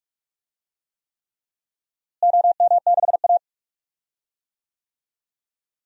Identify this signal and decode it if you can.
Morse code, heard as a keyed tone, about 43 wpm, 700 Hz. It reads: OM6A
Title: DXCC-100 Call Signs - 4 Characters - Repeated 3x - 45wpm